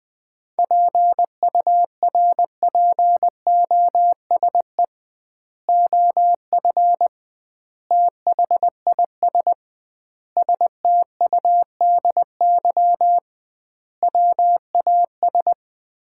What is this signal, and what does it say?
Morse code, heard as a keyed tone, about 21 wpm, 700 Hz. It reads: PURPOSE OF THIS STUDY WAS
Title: 2x Word-Spacing - Sets of 5 Words - 20wpm